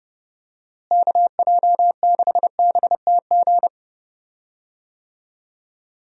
Morse code, heard as a keyed tone, about 30 wpm, 700 Hz. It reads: KJ6BTG
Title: USA Call Signs - 6 Characters - Repeated 3x - 30wpm